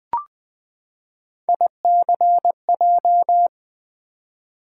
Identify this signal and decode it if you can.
Morse code, heard as a keyed tone, about 20 wpm, 700 Hz. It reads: ICJ
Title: Three Letter-Number Combinations - 20wpm